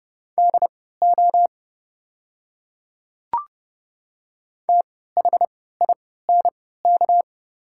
Morse code, heard as a keyed tone, about 30 wpm, spaced as for 17 wpm, 700 Hz. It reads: DO THINK
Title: 3x Character Spacing - Top 200 Words - 30wpm